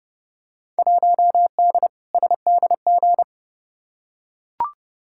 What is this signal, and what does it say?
Morse code, heard as a keyed tone, about 30 wpm, 700 Hz. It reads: 1D SDG
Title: Field Day Exchanges - 30wpm